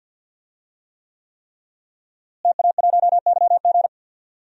Morse code, heard as a keyed tone, about 46 wpm, 700 Hz. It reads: TA1YG
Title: DXCC-100 Call Signs - 5 Characters - Repeated 3x - 50wpm